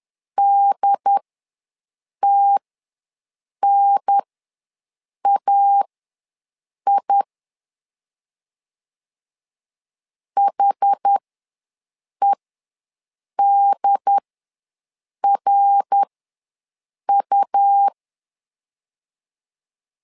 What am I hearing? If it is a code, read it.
Morse code, heard as a keyed tone, about 11 wpm, spaced as for 6 wpm, 800 Hz. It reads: DTNAI HEDRU
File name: LEZIONI / 3 MWRO / 3 MWR ws 10wpm 2151